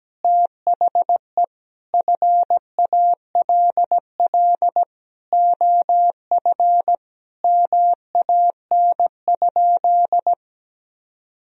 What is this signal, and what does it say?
Morse code, heard as a keyed tone, about 17 wpm, 700 Hz. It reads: THE FALL OF MAN?
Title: Sentences from Top 1000 Words - Review Part 2 - 17wpm